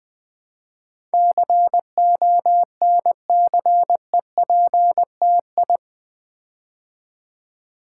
Morse code, heard as a keyed tone, about 20 wpm, 700 Hz. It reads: CONCEPTI
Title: Build A Word - 4 Letters At A Time - 10-Letter Words - 20wpm